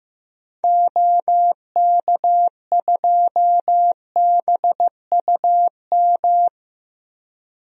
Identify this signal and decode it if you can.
Morse code, heard as a keyed tone, about 15 wpm, 700 Hz. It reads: OK2BUM
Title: DXCC-100 Call Signs - 6 Characters - 15wpm